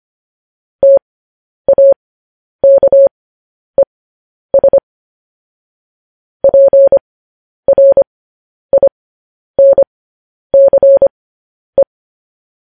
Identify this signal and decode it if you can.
Morse code, heard as a keyed tone, about 25 wpm, spaced as for 10 wpm, 550 Hz. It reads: TAKES PRINCE
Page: news headlines in morse code at 10 wpm